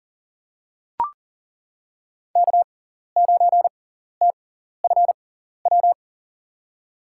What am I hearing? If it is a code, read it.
Morse code, heard as a keyed tone, about 40 wpm, spaced as for 14 wpm, 700 Hz. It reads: K9TFW
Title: USA Call Signs - 6x Character Spacing - 40wpm